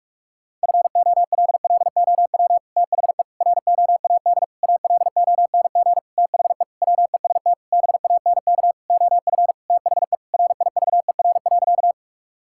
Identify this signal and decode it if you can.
Morse code, heard as a keyed tone, about 46 wpm, 700 Hz. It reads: FOLLOW THE ROAD ALONG THE WEST BANK OF THE RIVER.
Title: Sentences from Top 1000 Words - Review Part 6 - 45wpm